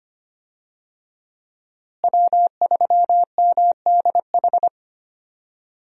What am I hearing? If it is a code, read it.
Morse code, heard as a keyed tone, about 25 wpm, 700 Hz. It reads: W3MDH
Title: USA Call Signs - 5 Characters - Repeated 3x - 25wpm